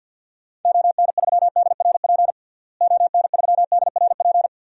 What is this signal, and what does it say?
Morse code, heard as a keyed tone, about 50 wpm, 700 Hz. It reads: ON3DRP ON3DRP
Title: DXCC-100 Call Signs - 6 Characters - Repeated 2x - 50wpm